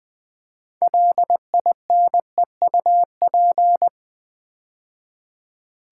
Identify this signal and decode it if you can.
Morse code, heard as a keyed tone, about 20 wpm, 700 Hz. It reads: LINEUP
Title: Build A Word - 3 Letters At A Time - 6-Letter Words - 20wpm